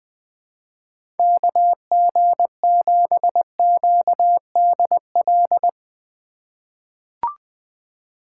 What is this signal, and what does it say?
Morse code, heard as a keyed tone, about 20 wpm, 700 Hz. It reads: KG7QDL
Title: USA Call Signs - 6 Characters - 3x Character Spacing - 20wpm